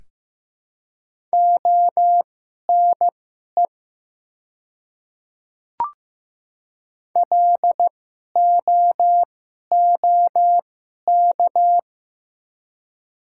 Morse code, heard as a keyed tone, about 15 wpm, spaced as for 11 wpm, 700 Hz. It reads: ONE LOOK
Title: 2x Character Spacing - Top 300 Words - 15wpm